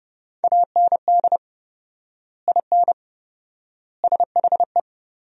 Morse code, heard as a keyed tone, about 30 wpm, 700 Hz. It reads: AND IN SHE
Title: 4x Word-Spacing - Sentences from Top 200 Words - 30wpm